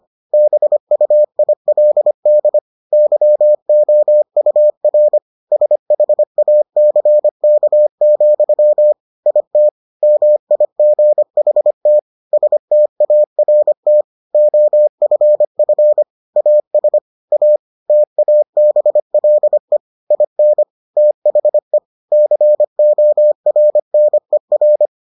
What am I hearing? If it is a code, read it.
Morse code, heard as a keyed tone, about 25 wpm, 600 Hz. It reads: BUILD YOUR SHACK, IT MIGHT START OFF AS A TABLE IN THE CORNER